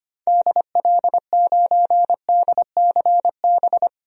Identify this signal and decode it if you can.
Morse code, heard as a keyed tone, about 25 wpm, 700 Hz. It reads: DL9DCB
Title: DXCC-100 Call Signs - 6 Characters - 25wpm